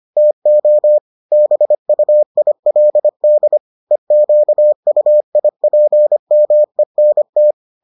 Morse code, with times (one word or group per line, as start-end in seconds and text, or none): 0.2-1.0 TO
1.3-3.6 BUILD
3.9-7.5 EQUIPMENT